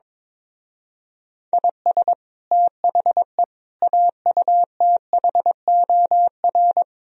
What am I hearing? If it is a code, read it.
Morse code, heard as a keyed tone, about 23 wpm, 700 Hz. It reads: IS THE AUTHOR
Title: Speed-Racing - Sets of 4 Words 22wpm